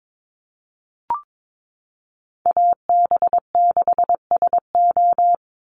Morse code, heard as a keyed tone, about 22 wpm, 700 Hz. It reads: AB6SO
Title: NA SOTA Chaser Callsigns 22wpm